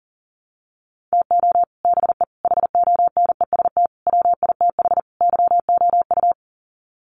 Morse code, heard as a keyed tone, about 40 wpm, 700 Hz. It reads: TO BE HONEST WITH YOU
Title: Sets of 5 Words Encore 40wpm